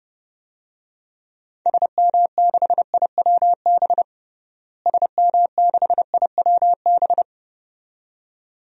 Morse code, heard as a keyed tone, about 32 wpm, 700 Hz. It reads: SM6IWB SM6IWB
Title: DXCC-100 Call Signs - 6 Characters - Repeated 2x - 30wpm